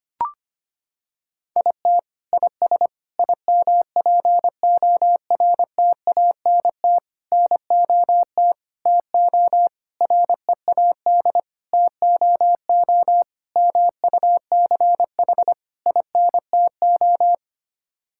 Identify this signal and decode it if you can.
Morse code, heard as a keyed tone, about 25 wpm, 700 Hz. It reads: IT IS IMPORTANT NOT TO READ TOO MUCH INTO
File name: Sets of 9 Words - 25wpm